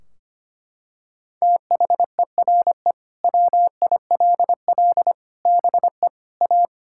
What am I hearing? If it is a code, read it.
Morse code, heard as a keyed tone, about 25 wpm, 700 Hz. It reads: THERE WILL BE A
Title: Sentences from Top 1600 Words 25wpm